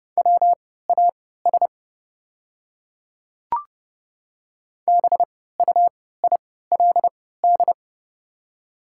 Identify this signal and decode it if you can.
Morse code, heard as a keyed tone, about 30 wpm, spaced as for 17 wpm, 690 Hz. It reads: WAS BUILD
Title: 3x Character Spacing - Top 300 Words - 30wpm